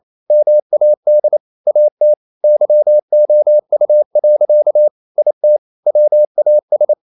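Morse code, heard as a keyed tone, about 28 wpm, 600 Hz. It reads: MAD AT YOU. IT WAS
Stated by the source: Morse code practice at 28 wpm, text of Ken Follett's novel Never in English, from CW med Gnister.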